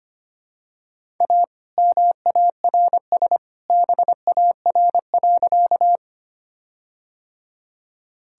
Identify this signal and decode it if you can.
Morse code, heard as a keyed tone, about 25 wpm, 700 Hz. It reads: A MARS BAR.
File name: practice / 25wpm / All-Age Appropriate Jokes - 25wpm